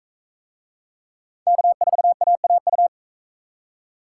Morse code, heard as a keyed tone, about 44 wpm, 700 Hz. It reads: K4AAU